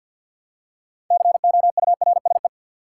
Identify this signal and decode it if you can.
Morse code, heard as a keyed tone, about 52 wpm, 700 Hz. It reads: COURSE
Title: Six Letter Words - 50wpm